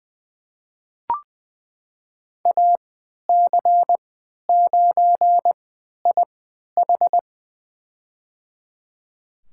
Morse code, heard as a keyed tone, about 20 wpm, spaced as for 11 wpm, 700 Hz. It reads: AC9IH
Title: USA Call Signs - 5 Characters - 3x Character Spacing - 20wpm